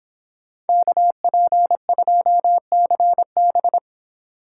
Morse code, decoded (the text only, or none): KP2CB